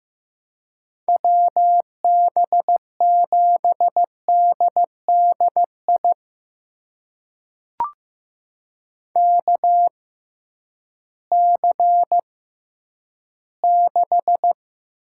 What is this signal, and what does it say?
Morse code, heard as a keyed tone, about 15 wpm, spaced as for 10 wpm, 700 Hz. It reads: WB7DDI KC6
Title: USA Call Signs - 6 Characters - 6x Character Spacing - 15wpm